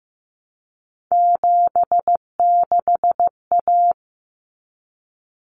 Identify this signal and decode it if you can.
Morse code, heard as a keyed tone, about 15 wpm, 700 Hz. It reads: 76A
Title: Common Cut Numbers - 3-Digit 15wpm